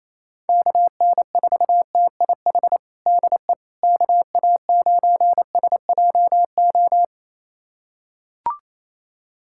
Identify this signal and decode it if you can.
Morse code, heard as a keyed tone, about 27 wpm, 700 Hz. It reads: KN4TIH DE KA9SJO
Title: USA Callsign DE Callsign - 6 Characters - 28wpm